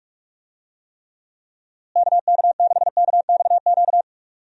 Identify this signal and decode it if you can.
Morse code, heard as a keyed tone, about 46 wpm, 700 Hz. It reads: KK6KXQ
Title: USA Call Signs - 6 Characters - Repeated 3x - 45wpm